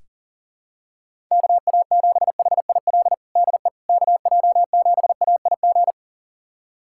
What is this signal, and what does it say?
Morse code, heard as a keyed tone, about 40 wpm, 700 Hz. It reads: KA7HIL DE KJ7AIG
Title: USA Callsign DE Callsign - 6 Characters - 40wpm